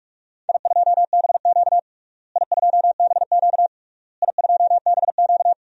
Morse code, heard as a keyed tone, about 42 wpm, 700 Hz. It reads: I2BQ I2BQ I2BQ
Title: DXCC-100 Call Signs - 4 Characters - Repeated 3x - 45wpm